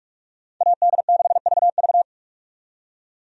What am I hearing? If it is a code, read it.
Morse code, heard as a keyed tone, about 45 wpm, 700 Hz. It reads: AD6VV
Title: USA Call Signs - 5 Characters - 45wpm